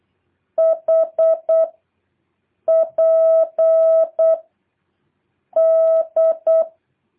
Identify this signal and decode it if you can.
Morse code, heard as a keyed tone, about 8 wpm, 650 Hz. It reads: H P D
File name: jh8cbh / 2023cw